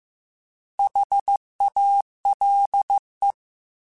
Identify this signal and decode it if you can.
Morse code, heard as a keyed tone, about 15 wpm, 800 Hz. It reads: HALE
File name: Alphabetical / Alphabetical mp3s / Hs @13